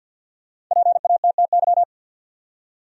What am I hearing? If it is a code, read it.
Morse code, heard as a keyed tone, about 52 wpm, 700 Hz. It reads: PATTY
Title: USA Common Names - Max 5 Letters - 50wpm